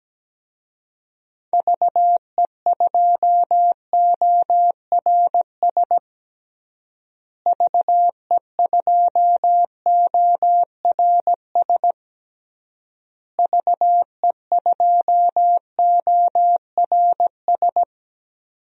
Morse code, heard as a keyed tone, about 17 wpm, 700 Hz. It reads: VE2ORS VE2ORS VE2ORS
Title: DXCC-100 Call Signs - 6 Characters - Repeated 3x - 17wpm